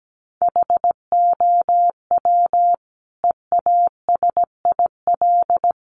Morse code, heard as a keyed tone, about 17 wpm, 700 Hz. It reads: HOW EASIL
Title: Sentences from Top 900 Words 17wpm